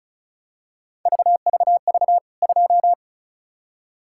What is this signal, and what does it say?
Morse code, heard as a keyed tone, about 34 wpm, 700 Hz. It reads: VVV 2
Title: K1USN SST QSO Practice 35wpm